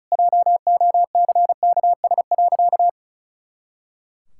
Morse code, heard as a keyed tone, about 36 wpm, 700 Hz. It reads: JOCKS.
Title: Pangram Challenge v2 - 35wpm